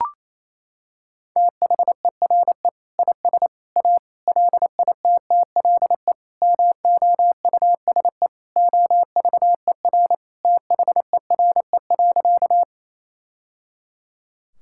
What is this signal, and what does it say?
Morse code, heard as a keyed tone, about 28 wpm, 700 Hz. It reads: THERE IS A LITTLE MOUSE OVER THERE.
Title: Sentences from Top 1000 Words - Review Part 5 - 28wpm